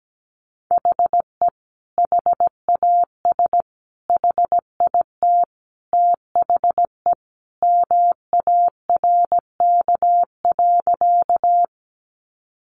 Morse code, heard as a keyed tone, about 17 wpm, 700 Hz. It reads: HE HAS HIT THE MARK.
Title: Sentences from Top 1200 Words 17wpm